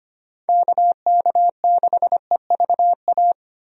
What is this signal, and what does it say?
Morse code, heard as a keyed tone, about 25 wpm, 700 Hz. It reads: KK6EVA
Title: USA Call Signs - 6 Characters - Repeated 2x - 25wpm